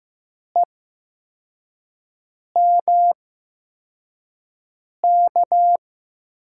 Morse code, heard as a keyed tone, about 15 wpm, spaced as for 4 wpm, 700 Hz. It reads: EMK
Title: USA Call Signs - 6 Characters - 8x Character Spacing - 15wpm